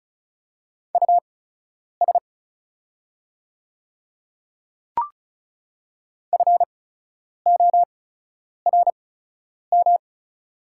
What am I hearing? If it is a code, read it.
Morse code, heard as a keyed tone, about 36 wpm, spaced as for 10 wpm, 700 Hz. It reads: US FORM
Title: 8x Character Spacing - Top 200 Words - 35wpm